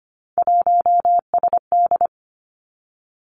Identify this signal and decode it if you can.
Morse code, heard as a keyed tone, about 25 wpm, 700 Hz. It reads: 1SD
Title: Three Challenging Letter-Number Combinations 25wpm